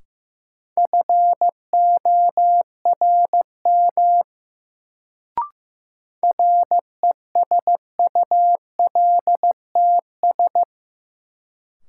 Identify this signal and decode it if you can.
Morse code, heard as a keyed tone, about 15 wpm, 700 Hz. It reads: FORM RESULTS
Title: Sets of 4 Words - New Word Drill - 15wpm